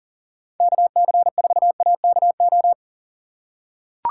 Morse code, heard as a keyed tone, about 40 wpm, 700 Hz. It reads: KC4AKO
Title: USA Call Signs - 6x Character Spacing - 40wpm